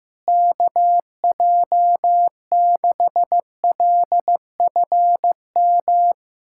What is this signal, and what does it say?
Morse code, heard as a keyed tone, about 16 wpm, 700 Hz. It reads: KJ6LFM
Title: USA Callsign DE Callsign - 6 Characters - 15wpm